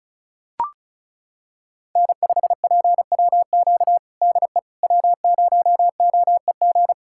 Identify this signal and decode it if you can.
Morse code, heard as a keyed tone, about 36 wpm, 700 Hz. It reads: N5PWQ DE W0OEG
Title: USA Callsign DE Callsign - 5 Characters - 35wpm